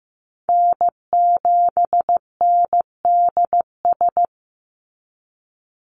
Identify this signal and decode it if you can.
Morse code, heard as a keyed tone, about 15 wpm, 700 Hz. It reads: N7NDS
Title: USA Call Signs 15wpm